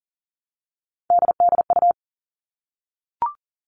Morse code, heard as a keyed tone, about 40 wpm, 700 Hz. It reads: DDU
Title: All Cut Numbers - 3-Digit 40wpm